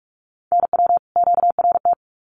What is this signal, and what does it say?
Morse code, heard as a keyed tone, about 44 wpm, 700 Hz. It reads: NW QRT
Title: Common QSO Phrases 45wpm